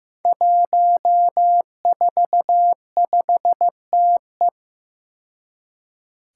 Morse code, heard as a keyed tone, about 15 wpm, 700 Hz. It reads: 145TE
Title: Five Challenging Letter-Number Combinations 15wpm